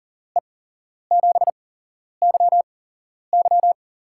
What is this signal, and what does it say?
Morse code, heard as a keyed tone, about 40 wpm, spaced as for 11 wpm, 700 Hz. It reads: E7YY
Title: USA Call Signs - 5 Characters - 8x Character Spacing - 40wpm